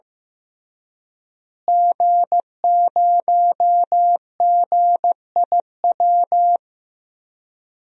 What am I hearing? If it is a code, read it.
Morse code, heard as a keyed tone, about 16 wpm, 700 Hz. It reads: G0GIW
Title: DXCC-100 Call Signs - 5 Characters - Repeated 3x - 15wpm